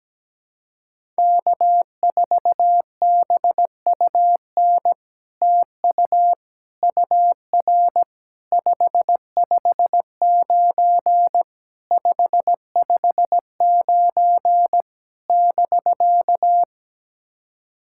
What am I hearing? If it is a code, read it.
Morse code, heard as a keyed tone, about 17 wpm, 700 Hz. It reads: K4BUN TU UR 559 559 <BK>
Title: SOTA QSO Practice 17wpm